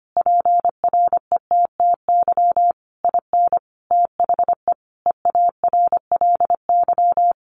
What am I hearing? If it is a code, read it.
Morse code, heard as a keyed tone, about 25 wpm, 700 Hz. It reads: PRETTY IN THE EARLY
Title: Sentences from Top 1300 Words 25wpm